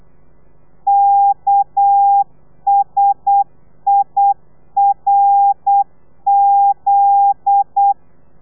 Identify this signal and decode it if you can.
Morse code, heard as a keyed tone, about 8 wpm, 800 Hz. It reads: KSIRZ
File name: telegrafia / i2tfj / Lez9 35